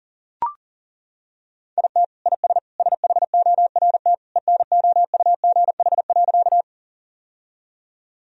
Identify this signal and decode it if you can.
Morse code, heard as a keyed tone, about 37 wpm, 700 Hz. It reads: IT IS SHORT ENOUGH.
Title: Sentences from Top 1000 Words - Review Part 2 - 40wpm